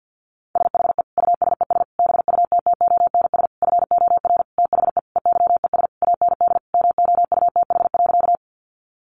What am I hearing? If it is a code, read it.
Morse code, heard as a keyed tone, about 50 wpm, 700 Hz. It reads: SHE USES BUTTONS FOR THE EYES AND MOUTH.
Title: Easy Phrases 50wpm